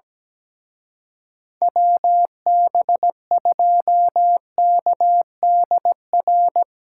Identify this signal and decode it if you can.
Morse code, heard as a keyed tone, about 19 wpm, 700 Hz. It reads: WB2KDR
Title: Speed Racing - USA 2x3 Call Signs 17wpm